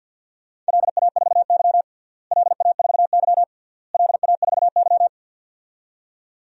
Morse code, heard as a keyed tone, about 51 wpm, 700 Hz. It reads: LA4Y LA4Y LA4Y
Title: DXCC-100 Call Signs - 4 Characters - Repeated 3x - 50wpm